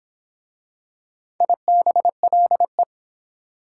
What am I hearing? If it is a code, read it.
Morse code, heard as a keyed tone, about 23 wpm, 700 Hz. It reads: IBLE